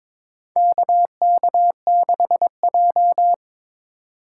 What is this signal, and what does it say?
Morse code, heard as a keyed tone, about 21 wpm, 700 Hz. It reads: KK6J
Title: USA Call Signs - 4 Characters - 2x Character Spacing - 22wpm